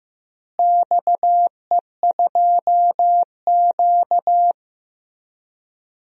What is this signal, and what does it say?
Morse code, heard as a keyed tone, about 15 wpm, 700 Hz. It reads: XE2Q
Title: DXCC-100 Call Signs - 4 Characters - Repeated 3x - 15wpm